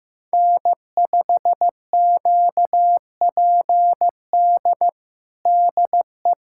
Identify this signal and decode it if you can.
Morse code, heard as a keyed tone, about 16 wpm, 700 Hz. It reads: N5QPD DE
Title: USA Callsign DE Callsign - 5 Characters - 15wpm